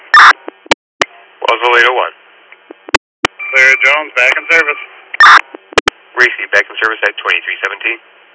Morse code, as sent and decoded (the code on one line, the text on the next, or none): none